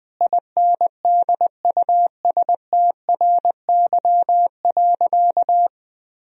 Morse code, heard as a keyed tone, about 20 wpm, 700 Hz. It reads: INDUSTRY.
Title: Sentences from Top 1000 Words - Review Part 4 - 20wpm